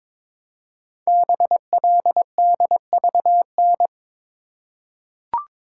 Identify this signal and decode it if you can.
Morse code, heard as a keyed tone, about 22 wpm, 700 Hz. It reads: BLDVN
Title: Code Groups - 5 Letter:Numbers - 22wpm